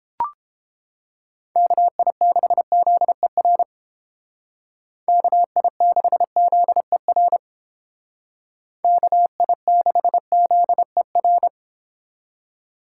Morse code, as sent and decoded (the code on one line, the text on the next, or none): -.- .. -.... --.. .-. / -.- .. -.... --.. .-. / -.- .. -.... --.. .-.
KI6ZER KI6ZER KI6ZER